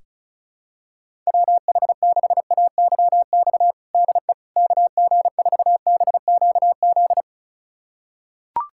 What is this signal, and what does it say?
Morse code, heard as a keyed tone, about 36 wpm, 700 Hz. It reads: WH6AYX DE KG4BQZ
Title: USA Callsign DE Callsign 35wpm